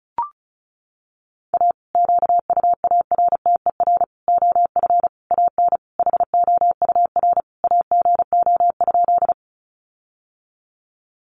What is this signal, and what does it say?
Morse code, heard as a keyed tone, about 34 wpm, 700 Hz. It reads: A QUARTER OF AN HOUR AGO?
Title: Sentences from Top 1300 Words 35wpm